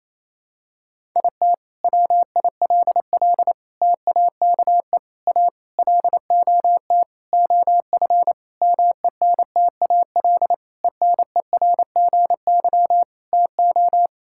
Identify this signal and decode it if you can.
Morse code, heard as a keyed tone, about 28 wpm, 700 Hz. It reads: IT WILL TAKE A LOT OF MENTAL ENERGY TO
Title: Sentences from Top 1000 Words - Review Part 6 - 28wpm